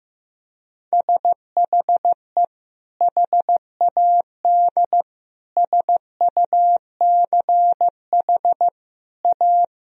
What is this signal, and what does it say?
Morse code, heard as a keyed tone, about 15 wpm, 700 Hz. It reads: SHE HAD SUCH A
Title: Sentences from Top 1000 Words - Review Part 1 - 15wpm